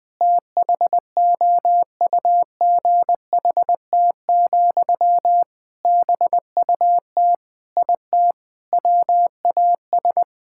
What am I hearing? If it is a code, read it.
Morse code, heard as a keyed tone, about 20 wpm, 700 Hz. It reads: THOUGHT, BUT IT WAS
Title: Sentences from Top 1000 Words - Review Part 6 - 20wpm